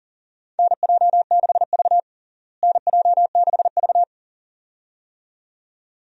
Morse code, heard as a keyed tone, about 37 wpm, 700 Hz. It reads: NJ6V NJ6V
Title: USA Call Signs - 4 Characters - Repeated 2x - 40wpm